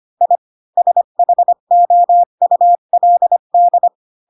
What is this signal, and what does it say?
Morse code, heard as a keyed tone, about 25 wpm, 700 Hz. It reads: I SHOULD
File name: Pinocchio / Pinocchio ch3 20 25 ik1hge